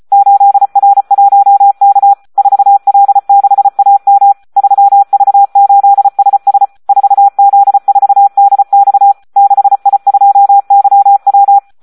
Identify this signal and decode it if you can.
Morse code, heard as a keyed tone, about 34 wpm, 800 Hz. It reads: 8R1K 4L6AM 3V8SS 4Z4DX 6I2YW
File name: Audio practica CW / SD 1 34